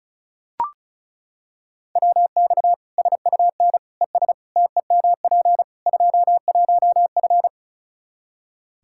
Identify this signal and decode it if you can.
Morse code, heard as a keyed tone, about 34 wpm, 700 Hz. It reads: WX SUN ES TEMP 21F
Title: Practice QSO - Exchange 2 - Info - 35wpm